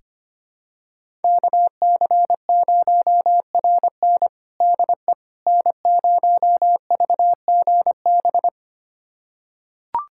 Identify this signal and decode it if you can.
Morse code, heard as a keyed tone, about 25 wpm, 700 Hz. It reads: KC0RN DE N0VGB
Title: USA Callsign DE Callsign 25wpm